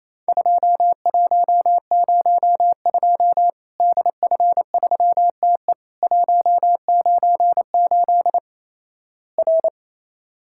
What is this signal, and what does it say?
Morse code, heard as a keyed tone, about 28 wpm, 700 Hz. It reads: 2102 DF3TE 198 R
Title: Worked All Europe DX Contest - QTC Exchange - 28wpm